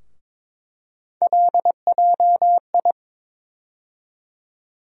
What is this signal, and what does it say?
Morse code, heard as a keyed tone, about 22 wpm, 700 Hz. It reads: LJI
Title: USA Call Sign Postfixes - 3 Letters 22wpm